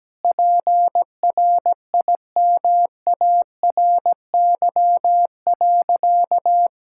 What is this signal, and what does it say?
Morse code, heard as a keyed tone, about 17 wpm, 700 Hz. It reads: PRIMARY.
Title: Sentences from Top 1000 Words - Review Part 5 - 17wpm